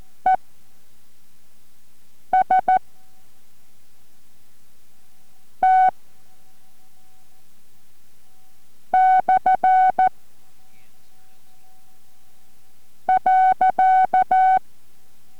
Morse code, heard as a keyed tone, about 14 wpm, spaced as for 4 wpm, 750 Hz. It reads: ES T / .